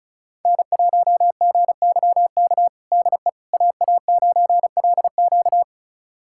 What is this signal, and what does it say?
Morse code, heard as a keyed tone, about 36 wpm, 700 Hz. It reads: N1GYK DE AA9LQ
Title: USA Callsign DE Callsign - 5 Characters - 35wpm